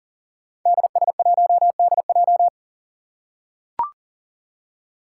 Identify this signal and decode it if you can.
Morse code, heard as a keyed tone, about 40 wpm, 700 Hz. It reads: DS1DJ
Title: Five Challenging Letter-Number Combinations 40wpm